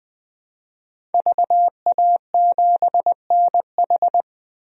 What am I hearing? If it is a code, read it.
Morse code, heard as a keyed tone, about 21 wpm, 700 Hz. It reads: VA7NH
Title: DXCC-100 Call Signs - 5 Characters - 20wpm